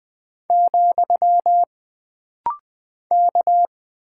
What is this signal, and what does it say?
Morse code, heard as a keyed tone, about 20 wpm, 700 Hz. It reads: , K
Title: Single Character (Letter:Number:Punctuation:Prosign) - Rapid-Fire - 20wpm